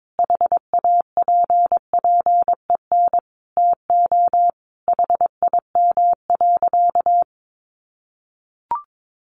Morse code, heard as a keyed tone, about 22 wpm, 700 Hz. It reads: HAPPEN TO HIM.
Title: Sentences from Top 400 Words 22wpm